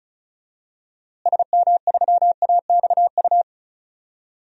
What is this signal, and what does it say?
Morse code, heard as a keyed tone, about 34 wpm, 700 Hz. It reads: SM3AXU
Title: DXCC-100 Call Signs - 6 Characters - Repeated 3x - 35wpm